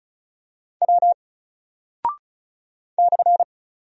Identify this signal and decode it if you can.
Morse code, heard as a keyed tone, about 36 wpm, 700 Hz. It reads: W /
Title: Single Character (Letter:Number:Punctuation:Prosign) 35wpm